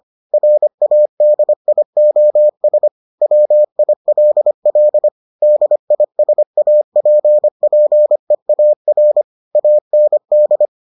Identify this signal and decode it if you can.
Morse code, heard as a keyed tone, about 25 wpm, 600 Hz. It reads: RADIOS WILL DISAPPEAR AND